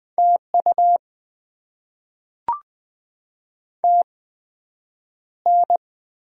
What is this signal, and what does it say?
Morse code, heard as a keyed tone, about 20 wpm, spaced as for 9 wpm, 700 Hz. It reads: TU TN
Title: Top 20 QSO Elements - 8x Character Spacing - 20wpm